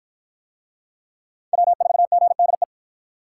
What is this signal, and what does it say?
Morse code, heard as a keyed tone, about 53 wpm, 700 Hz. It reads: W4GDE